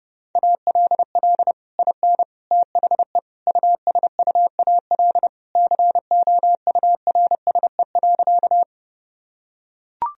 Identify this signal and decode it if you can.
Morse code, heard as a keyed tone, about 30 wpm, 700 Hz. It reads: ALL IN THE USUAL COURSE.
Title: Sentences from Top 1000 Words - Review Part 2 - 30wpm